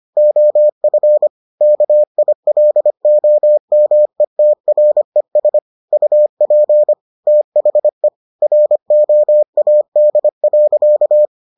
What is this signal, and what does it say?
Morse code, heard as a keyed tone, about 25 wpm, 600 Hz. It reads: OF KILOMETRES UP THE ROAD.